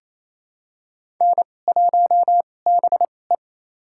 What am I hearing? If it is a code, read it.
Morse code, heard as a keyed tone, about 28 wpm, spaced as for 20 wpm, 700 Hz. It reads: N1BE
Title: USA Call Signs - 2x Character Spacing - 28wpm